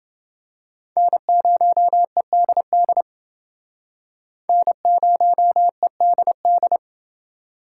Morse code, heard as a keyed tone, about 28 wpm, 700 Hz. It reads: N0EDD N0EDD